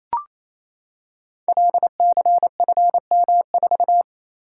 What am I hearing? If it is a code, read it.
Morse code, heard as a keyed tone, about 27 wpm, 700 Hz. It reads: LCFM4
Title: Code Groups - 5 Letter:Numbers - 28wpm